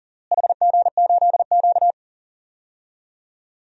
Morse code, heard as a keyed tone, about 40 wpm, 700 Hz. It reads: HG8Q
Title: DXCC-100 Call Signs - 4 Characters - Repeated 3x - 40wpm